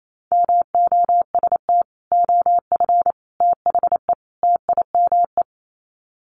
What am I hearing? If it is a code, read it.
Morse code, heard as a keyed tone, about 28 wpm, 700 Hz. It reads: MOST OF THE TIME
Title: Sets of 4 Words 28wpm